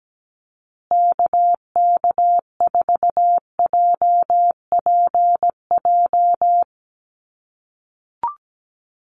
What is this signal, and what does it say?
Morse code, heard as a keyed tone, about 17 wpm, 700 Hz. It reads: KK4JPJ
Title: USA Call Signs Repeated 2x 17wpm